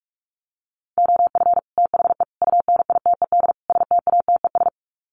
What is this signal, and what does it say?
Morse code, heard as a keyed tone, about 44 wpm, 700 Hz. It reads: OF THE UNITED STATES
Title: Sets of 4 Words 45wpm